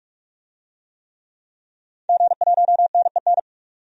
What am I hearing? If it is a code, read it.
Morse code, heard as a keyed tone, about 44 wpm, 700 Hz. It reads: G1NEN
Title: DXCC-100 Call Signs - 5 Characters - 45wpm